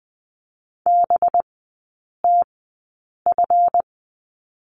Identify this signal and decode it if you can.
Morse code, heard as a keyed tone, about 20 wpm, spaced as for 8 wpm, 700 Hz. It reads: BTF
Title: 198 Lesson 32e - Letter k - Three Characters 20wpm